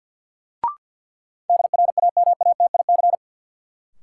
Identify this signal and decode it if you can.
Morse code, heard as a keyed tone, about 53 wpm, 700 Hz. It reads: DRAMATIC
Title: Sentences from Top 1300 Words - New Word Drill Alt - 50wpm